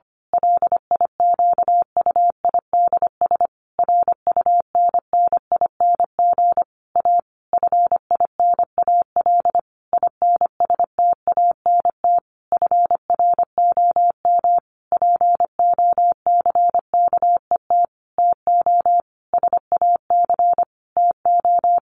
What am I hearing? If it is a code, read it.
Morse code, heard as a keyed tone, about 25 wpm, 700 Hz. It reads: LIQUIDS RUNNING A FINAL INSTANT FROM POCKET TO SAC TO
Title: A Sound Of Thunder 25wpm